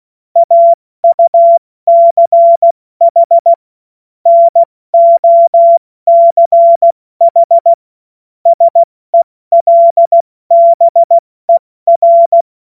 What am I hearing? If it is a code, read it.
Morse code, heard as a keyed tone, about 16 wpm, 680 Hz. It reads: AUCH NOCH SELBER
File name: Morsekurs / Witze / Witz 16 Januar 2024 70 F80